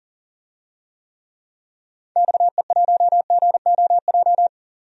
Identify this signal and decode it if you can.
Morse code, heard as a keyed tone, about 40 wpm, 700 Hz. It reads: XE1GOJ
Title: DXCC-100 Call Signs - 6 Characters - 40wpm